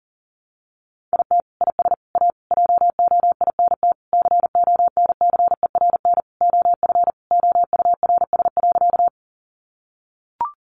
Morse code, heard as a keyed tone, about 40 wpm, 700 Hz. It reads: IT IS A JOINT CONCERN OF OURS.
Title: Sentences from Top 1300 Words 40wpm